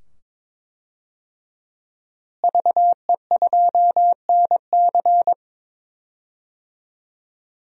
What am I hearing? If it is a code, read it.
Morse code, heard as a keyed tone, about 22 wpm, 700 Hz. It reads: VE2NC
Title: DXCC-100 Call Signs - 5 Characters - Repeated 2x - 22wpm